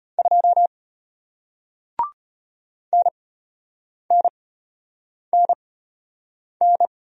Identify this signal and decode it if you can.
Morse code, heard as a keyed tone, about 33 wpm, spaced as for 8 wpm, 700 Hz. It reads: 2 NNNN